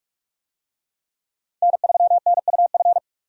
Morse code, heard as a keyed tone, about 40 wpm, 700 Hz. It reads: N3NUF